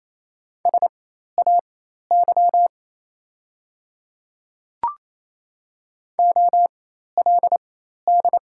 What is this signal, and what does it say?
Morse code, heard as a keyed tone, about 28 wpm, spaced as for 13 wpm, 690 Hz. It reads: SAY OLD
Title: 4x Character Spacing - Top 200 Words - 28wpm